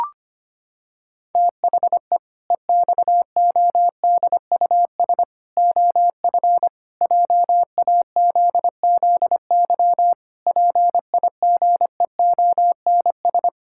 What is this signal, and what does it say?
Morse code, heard as a keyed tone, about 25 wpm, 700 Hz. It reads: THE EXODUS OF JAZZY PIGEONS
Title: Pangram Challenge v2 - 25wpm